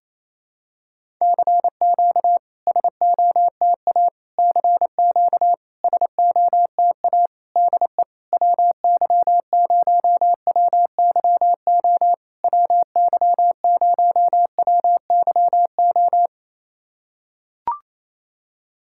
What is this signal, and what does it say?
Morse code, heard as a keyed tone, about 28 wpm, 700 Hz. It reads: CQ SOTA CQ SOTA DE WY0WYO WY0WYO
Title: SOTA QSO Practice 28wpm